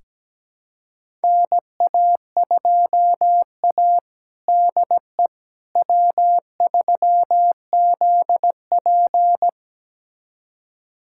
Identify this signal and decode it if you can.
Morse code, heard as a keyed tone, about 17 wpm, 700 Hz. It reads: NA2A DE W3ZP
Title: USA Callsign DE Callsign - 4 Characters - 17wpm